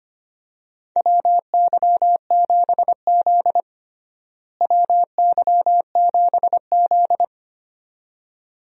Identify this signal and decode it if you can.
Morse code, heard as a keyed tone, about 25 wpm, 700 Hz. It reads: WY7Z WY7Z
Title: USA Call Signs - 4 Characters - Repeated 2x - 25wpm